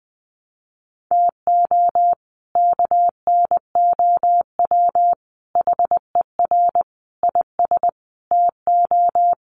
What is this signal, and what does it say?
Morse code, heard as a keyed tone, about 19 wpm, 700 Hz. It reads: TO KNOW HER IS TO
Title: Sentences from Top 400 Words 20wpm